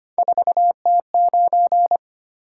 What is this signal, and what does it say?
Morse code, heard as a keyed tone, about 25 wpm, 700 Hz. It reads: 4T9
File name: Three Letter-Number Combinations - 25wpm